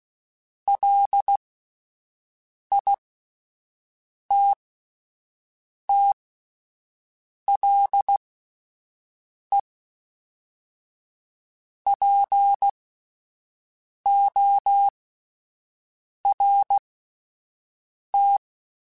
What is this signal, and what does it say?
Morse code, heard as a keyed tone, about 16 wpm, spaced as for 5 wpm, 800 Hz. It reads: LITTLEPORT